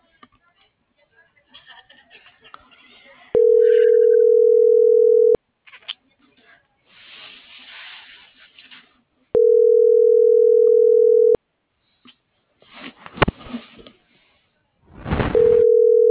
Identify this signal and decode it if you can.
Ringback tone (North America), 440 + 480 Hz.